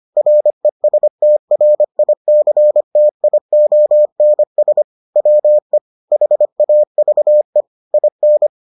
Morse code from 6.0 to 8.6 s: HAVE IN

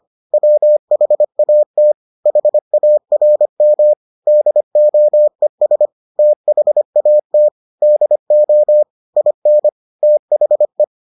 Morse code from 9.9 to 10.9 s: THE

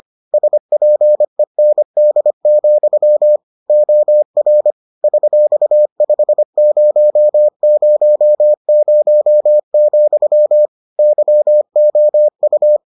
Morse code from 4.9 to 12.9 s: $5000, YOU

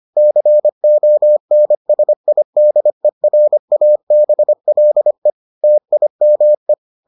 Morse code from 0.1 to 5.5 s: CONSIDERABLE

